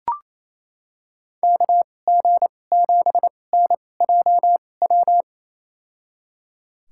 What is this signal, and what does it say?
Morse code, heard as a keyed tone, about 28 wpm, spaced as for 20 wpm, 700 Hz. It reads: KG7NJW
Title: USA Call Signs - 2x Character Spacing - 28wpm